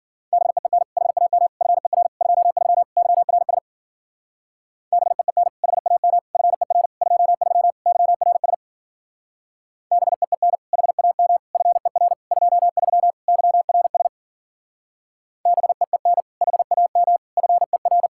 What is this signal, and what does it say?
Morse code, heard as a keyed tone, about 48 wpm, 700 Hz. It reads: BEEN HAM FER 23 YRS BEEN HAM FER 23 YRS BEEN HAM FER 23 YRS BEEN HAM FER
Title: Speed-Racing Common QSO Phrases Encore - 40wpm